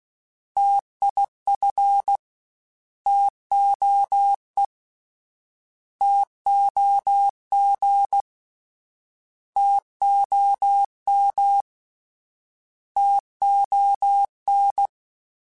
Morse code, heard as a keyed tone, about 16 wpm, 800 Hz. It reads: TIF TOE TOG TOM TON